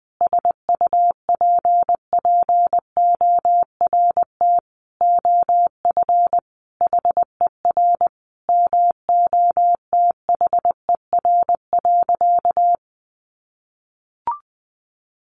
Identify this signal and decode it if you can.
Morse code, heard as a keyed tone, about 20 wpm, 700 Hz. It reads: SUPPORT OF HER MOTHER.
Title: Sentences from Top 800 Words 20wpm